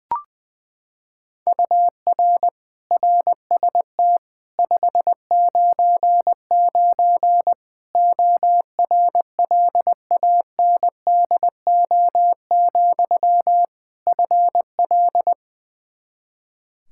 Morse code, heard as a keyed tone, about 20 wpm, 700 Hz. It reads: UR RST 599 ORLANDO, FL
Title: Practice QSO - Exchange 1 - Beginning - 20wpm